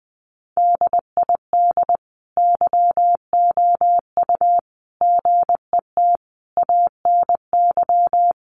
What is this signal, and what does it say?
Morse code, heard as a keyed tone, about 20 wpm, 700 Hz. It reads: DID YOU GET ANY
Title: Sentences from Top 400 Words 20wpm